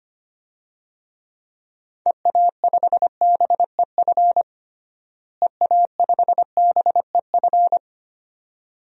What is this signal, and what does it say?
Morse code, heard as a keyed tone, about 25 wpm, 700 Hz. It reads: EA5BEF EA5BEF
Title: DXCC-100 Call Signs - 6 Characters - Repeated 2x - 25wpm